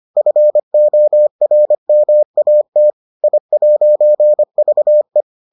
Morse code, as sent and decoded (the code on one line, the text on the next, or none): ..-. --- .-. -- .- - / .. .----. ...- .
FORMAT I'VE